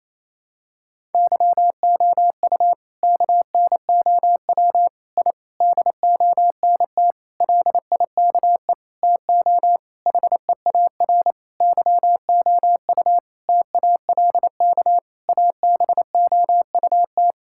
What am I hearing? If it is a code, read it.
Morse code, heard as a keyed tone, about 28 wpm, 700 Hz. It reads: YOU KNOW I DONT LIKE TO HEAR YOU TALK ABOUT
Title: Sentences from Top 1000 Words - Review Part 6 - 28wpm